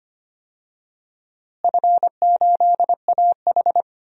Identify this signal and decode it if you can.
Morse code, heard as a keyed tone, about 25 wpm, 700 Hz. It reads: F8AH
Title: DXCC-100 Call Signs - 4 Characters - Repeated 3x - 25wpm